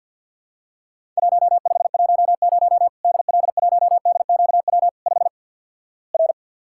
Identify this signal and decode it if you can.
Morse code, heard as a keyed tone, about 50 wpm, 700 Hz. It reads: 1510 DL1DQW 5 R
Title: Worked All Europe DX Contest - QTC Exchange - 50wpm